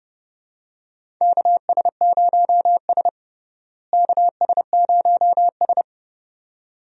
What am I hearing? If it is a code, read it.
Morse code, heard as a keyed tone, about 30 wpm, 700 Hz. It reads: KS0S KS0S
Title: USA Call Signs - 4 Characters - Repeated 2x - 30wpm